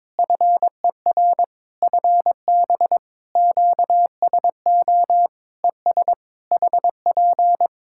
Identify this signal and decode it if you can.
Morse code, heard as a keyed tone, about 22 wpm, 700 Hz. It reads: FER FB QSO ES HP
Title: Practice QSO - Exchange 4 - Ending - 22wpm